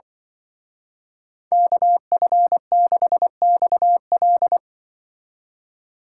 Morse code, heard as a keyed tone, about 23 wpm, 700 Hz. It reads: KF6XL
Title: Speed Racing - USA 2x2 Call Signs 20wpm